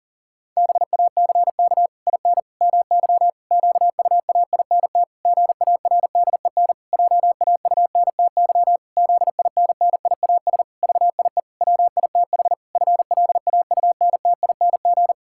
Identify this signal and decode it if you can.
Morse code, heard as a keyed tone, about 40 wpm, 700 Hz. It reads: BACK IN MY QUAINT GARDEN JAUNTY ZINNIAS VIE WITH FLAUNTING
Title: Pangram Challenge v2 - 40wpm